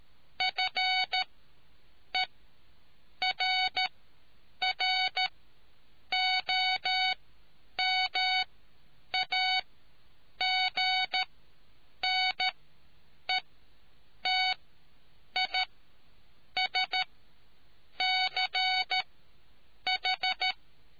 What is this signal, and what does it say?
Morse code, heard as a keyed tone, about 13 wpm, spaced as for 7 wpm, 750 Hz. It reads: FERROMAGNETISCH